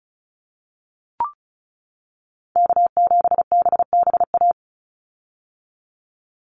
Morse code, heard as a keyed tone, about 34 wpm, 700 Hz. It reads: K7BBA
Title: USA Call Signs 35wpm